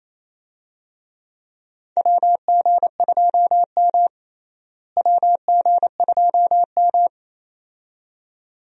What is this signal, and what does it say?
Morse code, heard as a keyed tone, about 27 wpm, 700 Hz. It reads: WG2M WG2M
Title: USA Call Signs - 4 Characters - Repeated 2x - 28wpm